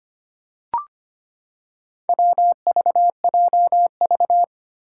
Morse code, heard as a keyed tone, about 25 wpm, 700 Hz. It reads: WVJV
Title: Four Challenging Letter-Number Combinations 25wpm